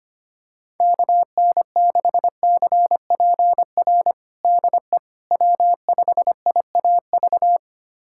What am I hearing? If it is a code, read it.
Morse code, heard as a keyed tone, about 25 wpm, 700 Hz. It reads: KN6CPR DE W5IAV
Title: USA Callsign DE Callsign 25wpm